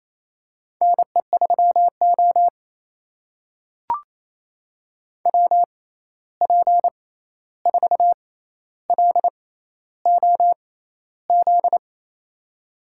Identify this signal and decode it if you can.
Morse code, heard as a keyed tone, about 28 wpm, spaced as for 13 wpm, 700 Hz. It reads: NE3O WP4LOZ
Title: USA Call Signs - 6x Character Spacing - 28wpm